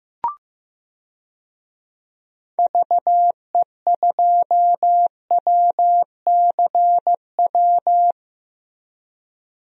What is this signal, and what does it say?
Morse code, heard as a keyed tone, about 15 wpm, 700 Hz. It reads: VE2WCW
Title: DXCC-100 Call Signs - 6 Characters - Repeated 3x - 15wpm